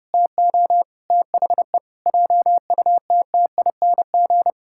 Morse code, heard as a keyed tone, about 32 wpm, 700 Hz. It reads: TO THE JUTTING
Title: Pangram Challenge v2 - 30wpm